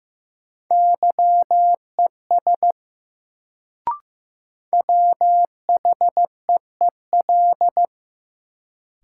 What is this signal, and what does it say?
Morse code, heard as a keyed tone, about 15 wpm, 700 Hz. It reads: YES WHEEL
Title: Sentences from Top 500 Words - New Word Drill - 15wpm